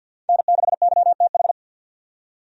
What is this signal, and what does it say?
Morse code, heard as a keyed tone, about 48 wpm, 700 Hz. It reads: N6YTH
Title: USA Call Signs - 5 Characters - 50wpm